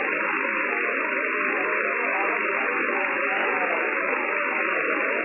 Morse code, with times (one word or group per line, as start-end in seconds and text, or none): none